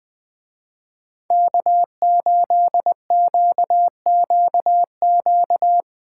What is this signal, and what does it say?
Morse code, heard as a keyed tone, about 19 wpm, 700 Hz. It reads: K8QQQ
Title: Speed Racing - USA 1x3 Call Signs 17wpm